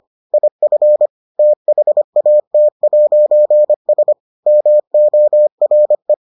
Morse code, heard as a keyed tone, about 25 wpm, 600 Hz. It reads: IF THAT'S MORE